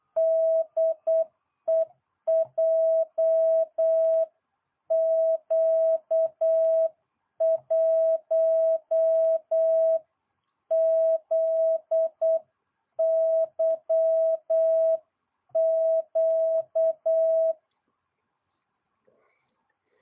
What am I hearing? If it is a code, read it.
Morse code, heard as a keyed tone, about 8 wpm, 650 Hz. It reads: DE JQ1ZYQ